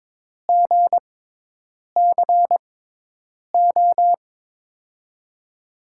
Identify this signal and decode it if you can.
Morse code, heard as a keyed tone, about 22 wpm, spaced as for 8 wpm, 700 Hz. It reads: GCO
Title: USA Call Signs - 6x Character Spacing - 22wpm